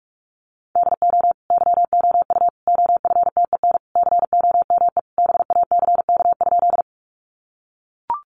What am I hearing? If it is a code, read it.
Morse code, heard as a keyed tone, about 44 wpm, 700 Hz. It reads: DO YOU OFTEN COME BACK?
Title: Sentences from Top 300 Words 45wpm